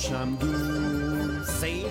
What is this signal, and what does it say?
Speech. There is no code.